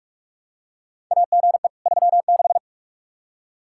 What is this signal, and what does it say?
Morse code, heard as a keyed tone, about 41 wpm, 700 Hz. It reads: AGE 36
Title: Practice QSO - Exchange 3 - More Info - 45wpm